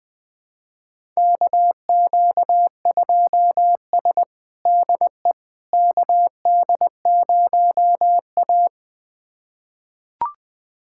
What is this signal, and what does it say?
Morse code, heard as a keyed tone, about 20 wpm, 700 Hz. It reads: KQ2S DE KD0A
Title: USA Callsign DE Callsign - 4 Characters - 20wpm